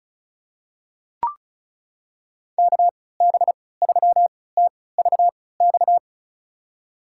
Morse code, heard as a keyed tone, about 36 wpm, spaced as for 20 wpm, 700 Hz. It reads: KB3TVX
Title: USA Call Signs - 3x Character Spacing - 35wpm